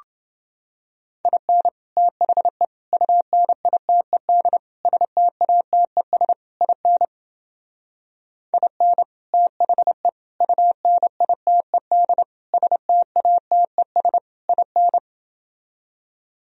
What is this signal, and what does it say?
Morse code, heard as a keyed tone, about 32 wpm, 700 Hz. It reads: IN THE UNITED STATES IN IN THE UNITED STATES IN